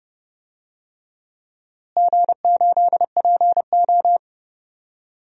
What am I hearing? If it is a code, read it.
Morse code, heard as a keyed tone, about 30 wpm, 700 Hz. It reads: G8PO
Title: DXCC-100 Call Signs - 4 Characters - Repeated 3x - 30wpm